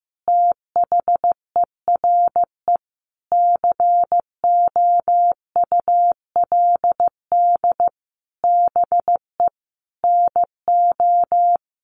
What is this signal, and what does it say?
Morse code, heard as a keyed tone, about 15 wpm, 700 Hz. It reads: THERE COULD BE NO
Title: Sentences from Top 1500 Words 15wpm